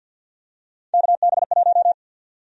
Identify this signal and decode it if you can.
Morse code, heard as a keyed tone, about 53 wpm, 700 Hz. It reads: KB1
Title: USA Call Sign Prefixes - 50wpm